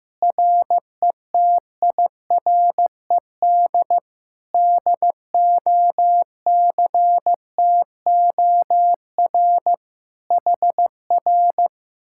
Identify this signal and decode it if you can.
Morse code, heard as a keyed tone, about 15 wpm, 700 Hz. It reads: RETIRED DOCTOR HR
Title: Speed-Racing Common QSO Phrases Encore - 15wpm